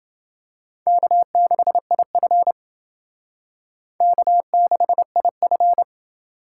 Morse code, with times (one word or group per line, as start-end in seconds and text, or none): 0.9-2.5 K6IF
4.0-5.8 K6IF